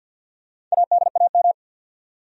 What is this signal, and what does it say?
Morse code, heard as a keyed tone, about 51 wpm, 700 Hz. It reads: ADAM
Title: USA Common Names - Max 5 Letters - 50wpm